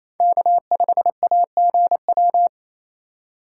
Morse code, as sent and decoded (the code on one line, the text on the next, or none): -.- ..... .- --. .--
K5AGW